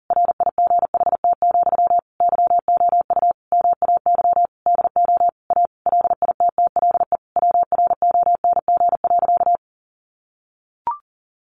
Morse code, heard as a keyed tone, about 40 wpm, 700 Hz. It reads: RIGHT, YOU MAY DO A LITTLE WRONG.
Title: Sentences from Top 900 Words 40wpm